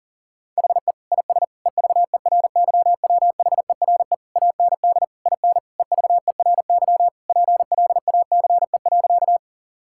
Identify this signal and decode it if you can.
Morse code, heard as a keyed tone, about 37 wpm, 700 Hz. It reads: HE IS EVERYWHERE AND IN EVERY PLACE.
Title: Sentences from Top 1600 Words 40wpm